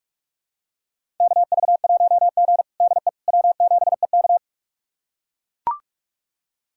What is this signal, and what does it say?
Morse code, heard as a keyed tone, about 44 wpm, 700 Hz. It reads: KV1G DE W7EK
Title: USA Callsign DE Callsign - 4 Characters - 45wpm